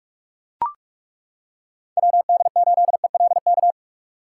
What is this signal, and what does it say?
Morse code, heard as a keyed tone, about 35 wpm, 700 Hz. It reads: WD8ELK
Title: Speed Racing - USA 2x3 Call Signs 30wpm